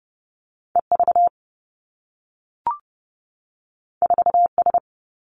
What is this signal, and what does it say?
Morse code, heard as a keyed tone, about 30 wpm, 700 Hz. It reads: E V 4 S
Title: Two Challenging Letter-Number Combinations 30wpm